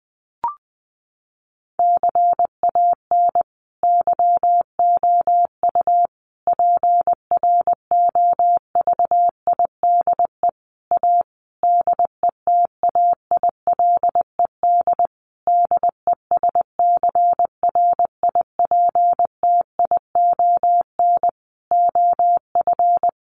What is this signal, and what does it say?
Morse code, heard as a keyed tone, about 20 wpm, 700 Hz. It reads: CAN YOU PROVIDE A DETAILED DESCRIPTION OF
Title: Sentences from Top 1500 Words 20wpm